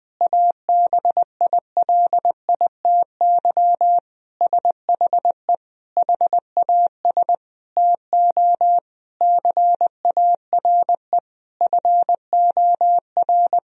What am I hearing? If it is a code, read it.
Morse code, heard as a keyed tone, about 20 wpm, 700 Hz. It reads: ABILITY SHE HAS TO CARE FOR
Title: Sentences from Top 1000 Words - Review Part 3 - 20wpm